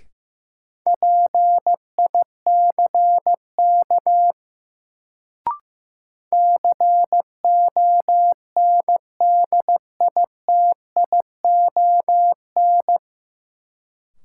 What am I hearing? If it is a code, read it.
Morse code, heard as a keyed tone, about 16 wpm, 700 Hz. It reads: PICK CONDITION
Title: Sentences from Top 1000 Words - New Word Drill - 15wpm